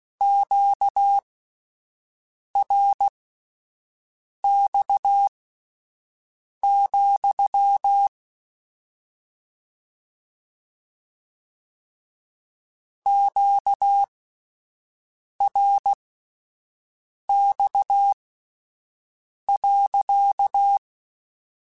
Morse code, heard as a keyed tone, about 16 wpm, spaced as for 5 wpm, 800 Hz. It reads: QRX, QRX.
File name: Q Calls / Q Calls-mp3s / Q Calls sppc @10